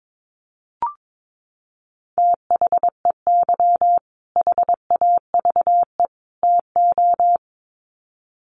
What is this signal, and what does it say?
Morse code, heard as a keyed tone, about 22 wpm, 700 Hz. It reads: THEY HAVE TO
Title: Sets of 3 Words 22wpm